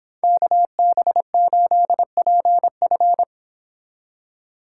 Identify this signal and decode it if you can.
Morse code, heard as a keyed tone, about 25 wpm, 700 Hz. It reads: KB8PF
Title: Speed Racing - USA 2x2 Call Signs 22wpm